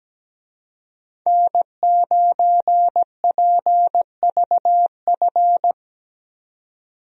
Morse code, heard as a keyed tone, about 17 wpm, 700 Hz. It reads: N9PVF
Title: USA Call Signs - 5 Characters - Repeated 2x - 17wpm